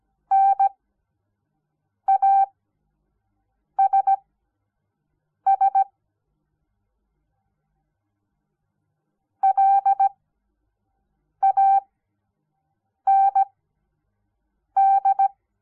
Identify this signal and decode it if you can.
Morse code, heard as a keyed tone, about 17 wpm, spaced as for 6 wpm, 780 Hz. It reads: NASS LAND
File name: Onlinelehrgang / morsen / DJ4UF Morsen06